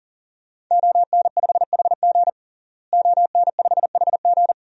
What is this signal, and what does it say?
Morse code, heard as a keyed tone, about 40 wpm, 700 Hz. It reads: ON5HG ON5HG
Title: DXCC-100 Call Signs - 5 Characters - Repeated 3x - 40wpm